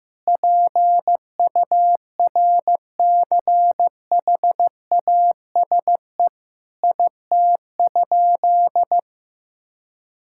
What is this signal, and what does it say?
Morse code, heard as a keyed tone, about 15 wpm, 700 Hz. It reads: PURCHASE IT?
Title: Sentences from Top 1000 Words - Review Part 5 - 15wpm